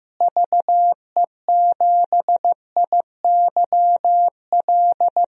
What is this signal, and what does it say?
Morse code, heard as a keyed tone, about 15 wpm, 700 Hz. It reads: VE7IYL
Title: DXCC-100 Call Signs - 6 Characters - Repeated 3x - 15wpm